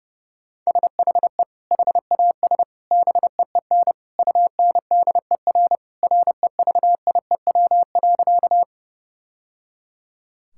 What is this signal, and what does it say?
Morse code, heard as a keyed tone, about 30 wpm, 700 Hz. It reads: SHE HAS BEEN UNDER REVIEW.
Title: Sentences from Top 1000 Words - Review Part 6 - 30wpm